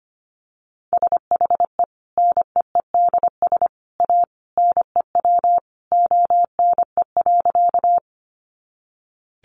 Morse code, heard as a keyed tone, about 25 wpm, 700 Hz. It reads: SHE NEEDS A NEW ONE.
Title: Additional Easy Phrases 25wpm